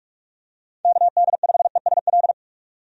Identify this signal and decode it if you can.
Morse code, heard as a keyed tone, about 46 wpm, 700 Hz. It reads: KD5ESL
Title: USA Call Signs - 6 Characters - 45wpm